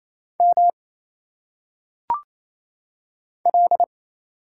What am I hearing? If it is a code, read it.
Morse code, heard as a keyed tone, about 27 wpm, 700 Hz. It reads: M L
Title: Single Letters - Phonetic Pronunciation 28wpm